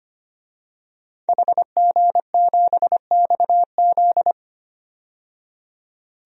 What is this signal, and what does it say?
Morse code, heard as a keyed tone, about 25 wpm, 700 Hz. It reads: HG7XZ
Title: DXCC-100 Call Signs - 5 Characters - 25wpm